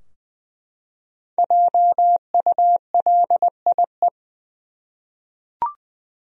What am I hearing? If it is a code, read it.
Morse code, heard as a keyed tone, about 20 wpm, 700 Hz. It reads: JULIE
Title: USA Common Names - 20wpm